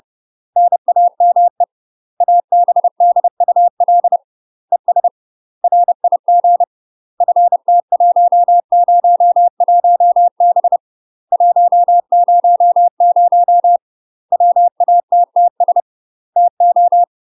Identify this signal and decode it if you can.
Morse code, heard as a keyed tone, about 30 wpm, 700 Hz. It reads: NAME ABDUL ES RIG FT101B 100 WATTS TO
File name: UK MASTER QSOs TRAINING / 30WPM / UK MASTER QSOs 0023